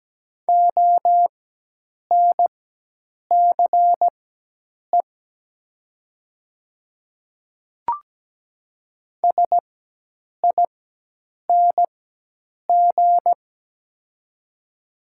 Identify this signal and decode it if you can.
Morse code, heard as a keyed tone, about 17 wpm, spaced as for 8 wpm, 700 Hz. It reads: ONCE SING
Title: 4x Character Spacing - Top 500 Words - 17wpm